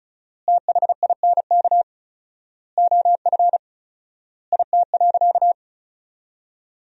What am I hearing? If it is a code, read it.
Morse code, heard as a keyed tone, about 36 wpm, 700 Hz. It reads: THINK OF IT.
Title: 4x Word-Spacing - Sentences from Top 200 Words - 35wpm